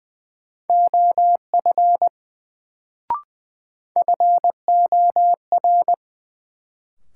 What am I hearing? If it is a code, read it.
Morse code, heard as a keyed tone, about 20 wpm, 700 Hz. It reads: OF FOR
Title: Sets of 2 Words - New Word Drill - 20wpm